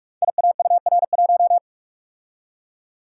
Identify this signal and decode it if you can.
Morse code, heard as a keyed tone, about 44 wpm, 700 Hz. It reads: IAUR1
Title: Code Groups - 5 Letter:Numbers - 45wpm